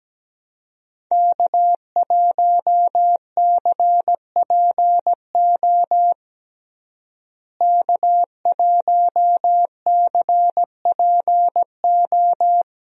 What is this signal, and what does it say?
Morse code, heard as a keyed tone, about 17 wpm, 700 Hz. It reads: K1CPO K1CPO
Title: USA Call Signs - 5 Characters - Repeated 2x - 17wpm